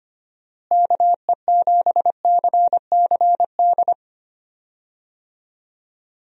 Morse code, heard as a keyed tone, about 25 wpm, 700 Hz. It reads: KE7CCD
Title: USA Call Signs - 6 Characters - Repeated 2x - 25wpm